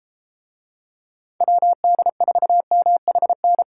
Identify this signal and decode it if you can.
Morse code, heard as a keyed tone, about 33 wpm, 700 Hz. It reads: WD4MHN